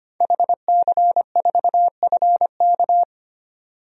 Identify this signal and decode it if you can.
Morse code, heard as a keyed tone, about 25 wpm, 700 Hz. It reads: HC4FK
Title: DXCC-100 Call Signs - 5 Characters - 25wpm